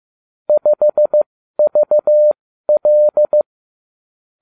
Morse code, heard as a keyed tone, about 15 wpm, spaced as for 12 wpm, 600 Hz. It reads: 5VL